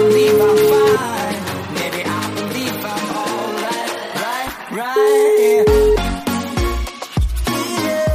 Ringback tone (Europe, 425 Hz), with noise.